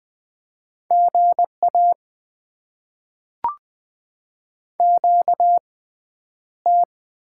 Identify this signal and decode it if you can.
Morse code, heard as a keyed tone, about 20 wpm, spaced as for 10 wpm, 700 Hz. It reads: GA QT
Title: Top 20 QSO Elements - 6x Character Spacing - 20wpm